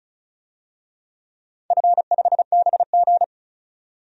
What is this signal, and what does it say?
Morse code, heard as a keyed tone, about 34 wpm, 700 Hz. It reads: F5BG
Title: DXCC-100 Call Signs - 4 Characters - 35wpm